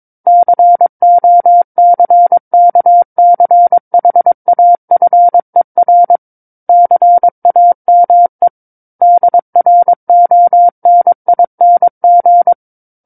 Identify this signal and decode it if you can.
Morse code, heard as a keyed tone, about 22 wpm, 700 Hz. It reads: COCKCHAFER CAME DRONING
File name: Book One / 13-Chapter 13 - 22 WPM